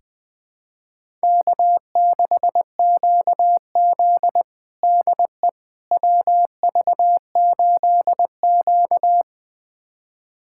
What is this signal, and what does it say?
Morse code, heard as a keyed tone, about 21 wpm, 700 Hz. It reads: K6QZ DE WV8Q
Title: USA Callsign DE Callsign - 4 Characters - 20wpm